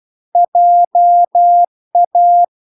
Morse code, heard as a keyed tone, about 12 wpm, 700 Hz. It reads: JA
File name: recordings / 12wpm0000